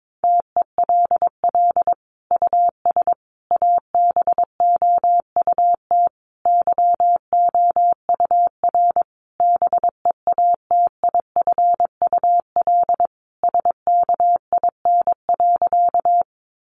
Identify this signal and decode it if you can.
Morse code, heard as a keyed tone, about 22 wpm, 700 Hz. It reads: TELL US ABOUT YOUR BEATIFUL SKIN.
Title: Sentences from Top 1300 Words 22wpm